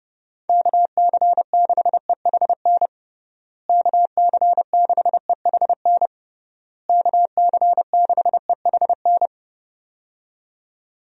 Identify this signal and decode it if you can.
Morse code, heard as a keyed tone, about 30 wpm, 700 Hz. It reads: KC6EHN KC6EHN KC6EHN
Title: USA Call Signs - 6 Characters - Repeated 3x - 30wpm